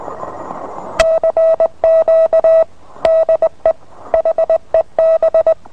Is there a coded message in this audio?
Yes: Morse code from 1.0 to 5.5 s, heard as a keyed tone, about 20 wpm, 650 Hz. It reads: CQ DE HEB